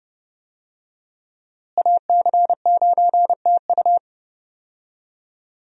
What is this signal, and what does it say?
Morse code, heard as a keyed tone, about 30 wpm, 700 Hz. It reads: AC9TU
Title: USA Call Signs - 5 Characters - Repeated 2x - 30wpm